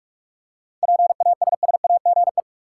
Morse code, heard as a keyed tone, about 45 wpm, 700 Hz. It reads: PASSAGE
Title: Seven Letter Words - 45wpm